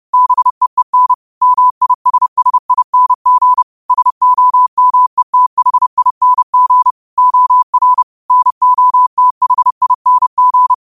Morse code, heard as a keyed tone, about 30 wpm, 1000 Hz. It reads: BEEN MISSING SOMETHING OR NOTHING
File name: qotdfiles / 30wpm